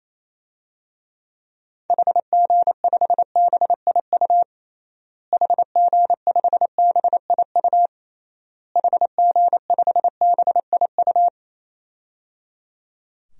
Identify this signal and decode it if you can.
Morse code, heard as a keyed tone, about 28 wpm, 700 Hz. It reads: HG5BIU HG5BIU HG5BIU
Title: DXCC-100 Call Signs - 6 Characters - Repeated 3x - 28wpm